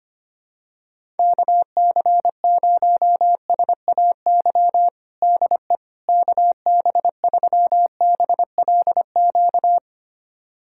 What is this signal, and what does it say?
Morse code, heard as a keyed tone, about 25 wpm, 700 Hz. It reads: KC0SAY DE KB3BLQ
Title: USA Callsign DE Callsign - 6 Characters - 25wpm